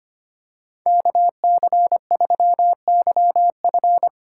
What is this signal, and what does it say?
Morse code, heard as a keyed tone, about 25 wpm, 700 Hz. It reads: KC3YF
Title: USA Call Signs - 5 Characters - 2x Character Spacing - 25wpm